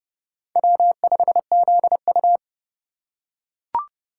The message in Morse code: W5ZU

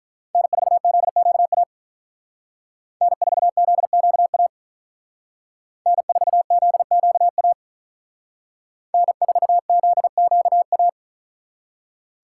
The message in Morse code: N4ZQA N4ZQA N4ZQA N4ZQA

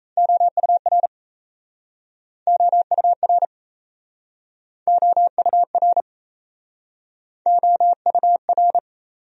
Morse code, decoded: OUR OUR OUR OUR

